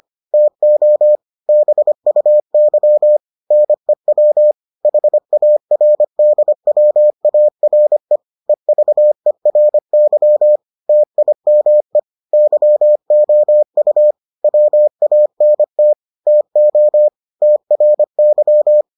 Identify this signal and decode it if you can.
Morse code, heard as a keyed tone, about 25 wpm, 600 Hz. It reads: TO BUY NEW HARDWARE EVERY TIME YOU WANT TO TRY